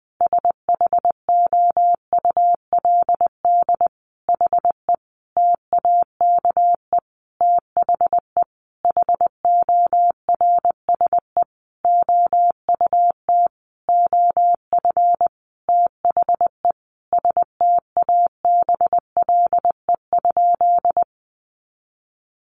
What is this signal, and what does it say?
Morse code, heard as a keyed tone, about 20 wpm, 700 Hz. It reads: SHOULD HE TAKE THE HORSE OUT OF THE STABLE?
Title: Sentences from Top 1500 Words 20wpm